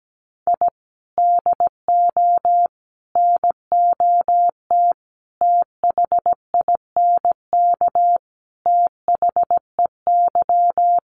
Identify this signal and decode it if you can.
Morse code, heard as a keyed tone, about 17 wpm, 700 Hz. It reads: I DO NOT THINK THEY
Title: Sets of 5 Words 17wpm